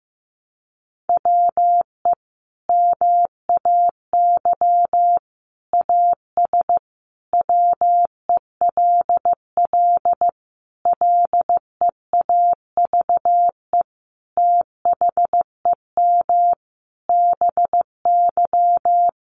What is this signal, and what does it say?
Morse code, heard as a keyed tone, about 14 wpm, 700 Hz. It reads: WE MAY AS WELL LEAVE THEM BY
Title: Sentences from Top 800 Words 15wpm